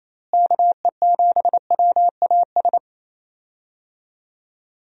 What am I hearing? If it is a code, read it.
Morse code, heard as a keyed tone, about 28 wpm, 700 Hz. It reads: KE7WAS
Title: USA Call Signs - 6 Characters - Repeated 3x - 28wpm